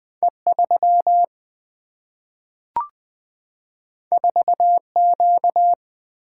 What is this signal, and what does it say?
Morse code, heard as a keyed tone, about 20 wpm, 700 Hz. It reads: E3 4Q
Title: Single-Letter Single-Number 20wpm